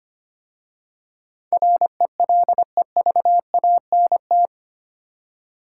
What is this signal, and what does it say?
Morse code, heard as a keyed tone, about 25 wpm, 690 Hz. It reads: RELEVANT
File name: 8 Letter Words - First 3 Letters Given - 25wpm